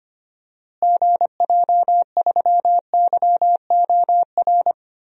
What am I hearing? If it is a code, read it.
Morse code, heard as a keyed tone, about 25 wpm, 700 Hz. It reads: GJ3YOR
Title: DXCC-100 Call Signs - 6 Characters - Repeated 3x - 25wpm